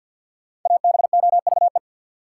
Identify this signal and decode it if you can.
Morse code, heard as a keyed tone, about 53 wpm, 700 Hz. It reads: ABOVE
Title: Sentences from Top 1000 Words - Review Part 2 - New Word Drill - 50wpm